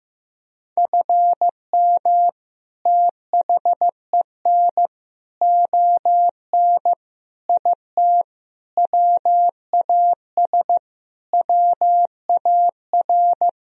Morse code, heard as a keyed tone, about 15 wpm, 700 Hz. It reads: FM THEN ON IT WAS WAR
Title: Sentences from Top 400 Words and Abbreviations 15wpm